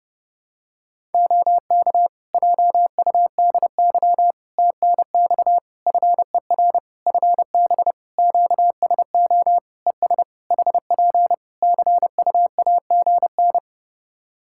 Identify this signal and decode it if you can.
Morse code, heard as a keyed tone, about 30 wpm, 700 Hz. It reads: OK JUDY TNX FER FB QSO ES HP CUAGN
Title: Practice QSO - Exchange 4 - Ending - 30wpm